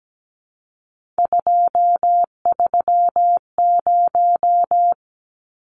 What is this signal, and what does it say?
Morse code, heard as a keyed tone, about 17 wpm, 700 Hz. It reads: 230
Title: Numbers - 3-Digit 17wpm